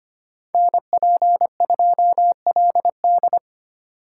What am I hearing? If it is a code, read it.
Morse code, heard as a keyed tone, about 30 wpm, 700 Hz. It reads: NP2LD
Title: Speed Racing - USA Call Signs 25wpm